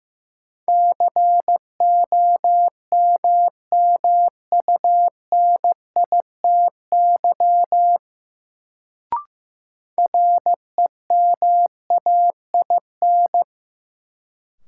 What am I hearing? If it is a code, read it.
Morse code, heard as a keyed tone, about 15 wpm, 700 Hz. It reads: COMMUNITY REMAIN
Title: Sentences from Top 1200 Words - New Word Drill Alt - 15wpm